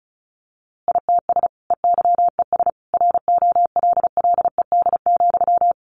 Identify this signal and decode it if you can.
Morse code, heard as a keyed tone, about 34 wpm, 700 Hz. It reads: ITS EYES ROLLED,
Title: A Sound Of Thunder 35wpm